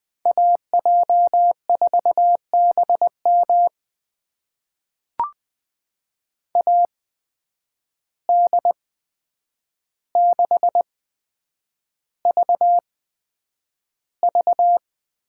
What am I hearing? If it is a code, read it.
Morse code, heard as a keyed tone, about 21 wpm, spaced as for 9 wpm, 700 Hz. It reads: AJ4BM AD6VV
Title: USA Call Signs - 5 Characters - 8x Character Spacing - 20wpm